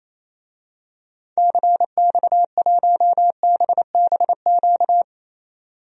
Morse code, heard as a keyed tone, about 27 wpm, 700 Hz. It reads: CX1BBQ
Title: DXCC-100 Call Signs - 6 Characters - 28wpm